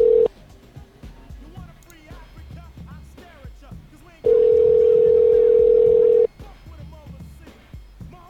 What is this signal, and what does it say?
Ringback tone (North America), 440 + 480 Hz.